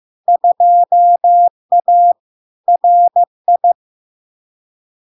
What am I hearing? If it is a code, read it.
Morse code, heard as a keyed tone, about 15 wpm, 700 Hz. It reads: 2A RI
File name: LARK-FD-2022-Log-in-CW-15wpm-700-Hz